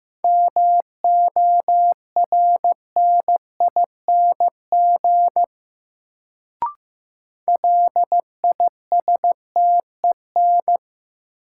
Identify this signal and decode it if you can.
Morse code, heard as a keyed tone, about 15 wpm, 700 Hz. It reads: MORNING LISTEN
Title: Sentences from Top 400 Words - New Word Drill - 15wpm